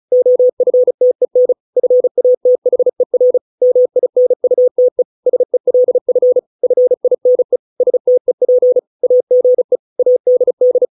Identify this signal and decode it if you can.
Morse code, heard as a keyed tone, about 34 wpm, 500 Hz. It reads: OFTEN FATHER MINUTE SELF FINE STEP AGE ADD